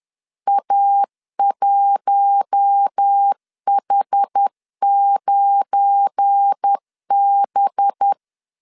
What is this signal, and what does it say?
Morse code, heard as a keyed tone, about 10 wpm, 800 Hz. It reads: A1H9B